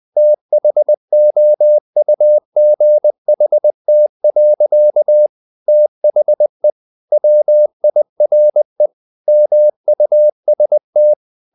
Morse code, heard as a keyed tone, about 20 wpm, 600 Hz. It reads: THOUGHT. THE WIRE MUST
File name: the old man and the sea / the old man and the sea part 1 20wpm